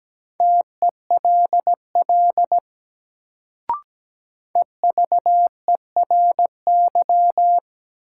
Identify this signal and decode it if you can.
Morse code, heard as a keyed tone, about 17 wpm, 700 Hz. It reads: TELL EVERY
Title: Sentences from Top 200 Words - New Word Drill - 17wpm